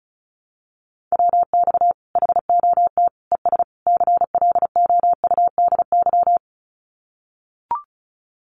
Morse code, heard as a keyed tone, about 34 wpm, 700 Hz. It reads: WX HOT ES CLOUDY
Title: Common QSO Phrases 35wpm